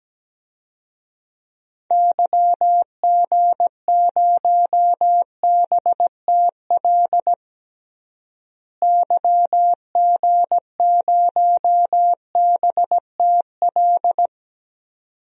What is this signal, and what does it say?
Morse code, heard as a keyed tone, about 17 wpm, 700 Hz. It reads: YG0BTL YG0BTL
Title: DXCC-100 Call Signs - 6 Characters - Repeated 2x - 17wpm